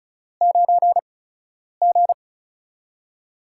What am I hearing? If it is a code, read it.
Morse code, heard as a keyed tone, about 36 wpm, spaced as for 10 wpm, 700 Hz. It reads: 9G
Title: USA Call Signs - 8x Character Spacing - 35wpm